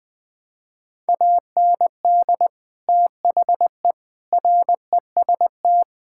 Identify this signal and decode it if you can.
Morse code, heard as a keyed tone, about 21 wpm, 700 Hz. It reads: AND THE REST
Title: Sentences with 8-Word Phrases - 20wpm